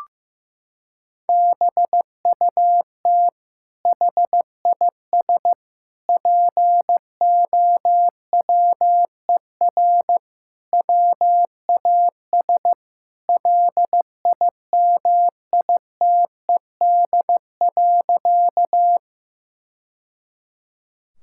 Morse code, heard as a keyed tone, about 15 wpm, 700 Hz. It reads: BUT HIS POWER WAS LIMITED.
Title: Sentences from Top 1000 Words - Review Part 6 - 15wpm